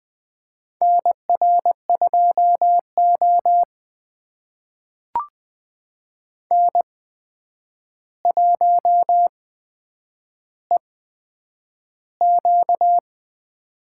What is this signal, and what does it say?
Morse code, heard as a keyed tone, about 20 wpm, spaced as for 9 wpm, 700 Hz. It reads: NR2O N1EQ
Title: USA Call Signs - 4 Characters - 8x Character Spacing - 20wpm